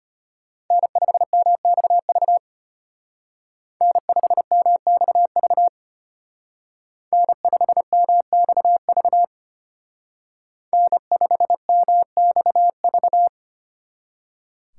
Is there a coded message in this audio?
Yes: Morse code from 0.7 to 13.3 s, heard as a keyed tone, about 32 wpm, 700 Hz. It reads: N5MXV N5MXV N5MXV N5MXV